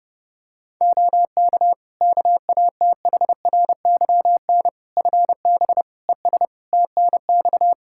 Morse code, heard as a keyed tone, about 30 wpm, 700 Hz. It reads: OK KATHRYN FB ES TNX
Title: K1USN SST QSO Practice 30wpm